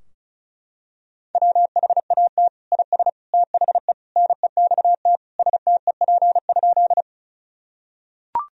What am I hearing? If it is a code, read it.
Morse code, heard as a keyed tone, about 36 wpm, 700 Hz. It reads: WHAT IS THE NEXT STEP?
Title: Sentences from Top 1000 Words - Review Part 2 - 35wpm